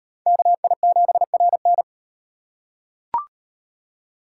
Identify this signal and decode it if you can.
Morse code, heard as a keyed tone, about 38 wpm, 700 Hz. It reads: KI7RN